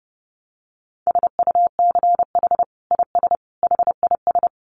Morse code, heard as a keyed tone, about 30 wpm, 700 Hz. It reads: SUCH IS HIS
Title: Sentences from Top 900 Words 30wpm